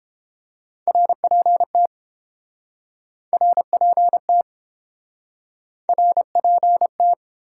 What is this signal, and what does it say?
Morse code, heard as a keyed tone, about 31 wpm, 700 Hz. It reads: RPT RPT RPT